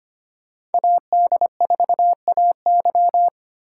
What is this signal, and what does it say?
Morse code, heard as a keyed tone, about 29 wpm, 700 Hz. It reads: AD4AY
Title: Speed Racing - USA 2x2 Call Signs 25wpm